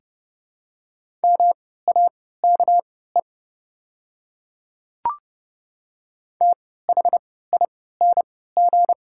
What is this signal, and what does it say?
Morse code, heard as a keyed tone, about 30 wpm, spaced as for 17 wpm, 700 Hz. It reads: MAKE THING
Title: 3x Character Spacing - Top 100 Words - 30wpm